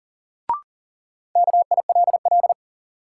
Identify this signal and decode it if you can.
Morse code, heard as a keyed tone, about 40 wpm, 700 Hz. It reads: KILL
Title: Sentences from Top 1400 Words - New Word Drill - 40wpm